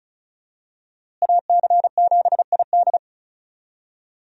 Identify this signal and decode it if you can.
Morse code, heard as a keyed tone, about 36 wpm, 700 Hz. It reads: AC7ID
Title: USA Call Signs - 5 Characters - Repeated 2x - 35wpm